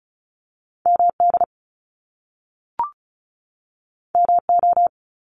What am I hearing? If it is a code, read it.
Morse code, heard as a keyed tone, about 36 wpm, 700 Hz. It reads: MD MO
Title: US States 35wpm